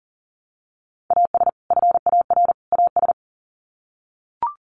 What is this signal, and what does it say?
Morse code, heard as a keyed tone, about 40 wpm, 700 Hz. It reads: AS FAR AS
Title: Sets of 3 Words 40wpm